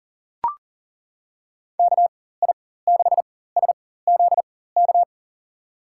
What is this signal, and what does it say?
Morse code, heard as a keyed tone, about 40 wpm, spaced as for 19 wpm, 700 Hz. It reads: KI6SZK
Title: USA Call Signs - 6 Characters - 4x Character Spacing - 40wpm